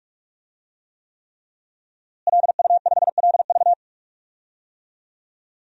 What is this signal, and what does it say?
Morse code, heard as a keyed tone, about 46 wpm, 700 Hz. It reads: LU5LV
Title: DXCC-100 Call Signs - 5 Characters - Repeated 3x - 45wpm